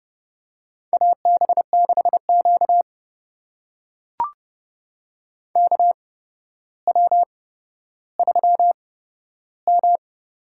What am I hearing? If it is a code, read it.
Morse code, heard as a keyed tone, about 30 wpm, spaced as for 13 wpm, 700 Hz. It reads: AB6Q KW3M
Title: USA Call Signs - 4 Characters - 8x Character Spacing - 30wpm